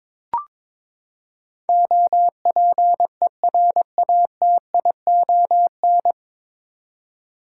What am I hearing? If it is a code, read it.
Morse code, heard as a keyed tone, about 22 wpm, 700 Hz. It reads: OPERATION
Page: Morse Code Ninja practice 22wpm